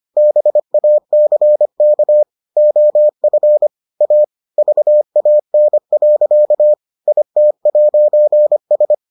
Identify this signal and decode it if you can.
Morse code, heard as a keyed tone, about 25 wpm, 600 Hz. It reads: BACK OF A VAN. IT'S